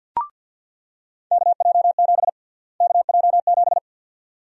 Morse code, heard as a keyed tone, about 49 wpm, 700 Hz. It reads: KJ7 KJ7
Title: USA Call Sign Prefixes - Repeated 2x - 50wpm